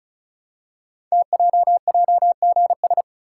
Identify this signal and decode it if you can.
Morse code, heard as a keyed tone, about 36 wpm, 700 Hz. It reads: TJJGS
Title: Code Groups - 5 Letters - 35wpm